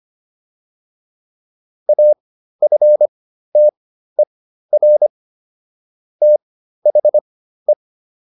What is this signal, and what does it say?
Morse code, heard as a keyed tone, about 25 wpm, spaced as for 13 wpm, 600 Hz. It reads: AFTER THE